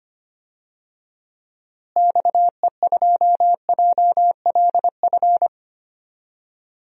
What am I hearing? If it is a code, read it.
Morse code, heard as a keyed tone, about 25 wpm, 700 Hz. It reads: XE2JLF
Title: DXCC-100 Call Signs - 6 Characters - Repeated 3x - 25wpm